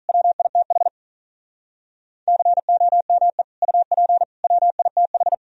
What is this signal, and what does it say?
Morse code, heard as a keyed tone, about 45 wpm, 700 Hz. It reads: WITH COME UP WITH